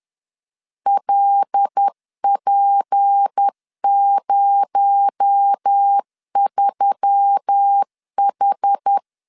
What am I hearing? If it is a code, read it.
Morse code, heard as a keyed tone, about 10 wpm, 800 Hz. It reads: LP03H
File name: LEZIONI / 7 GF9Z / 7 GF9Z 10wpm 2611